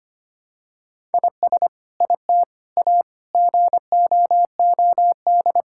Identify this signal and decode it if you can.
Morse code, heard as a keyed tone, about 25 wpm, 700 Hz. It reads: IS IT A GOOD
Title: Sentences from Top 1000 Words - Review Part 6 - 25wpm